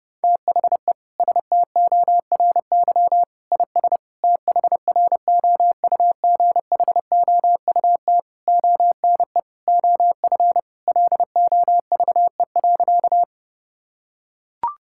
Morse code, heard as a keyed tone, about 30 wpm, 700 Hz. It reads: THE STORY IS THROUGHOUT ONE OF LOVE.
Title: Sentences from Top 1000 Words - Review Part 5 - 30wpm